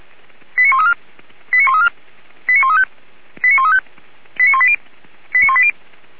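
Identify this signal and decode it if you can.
Five-tone selective call (ZVEI), heard as quick runs of five tones, tones of about 70 ms. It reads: 89134 89134 89135 89135 89180 89180
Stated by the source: ZVEI five-tone selective call received on an SDR, archived by Feuerwehr Orlamünde.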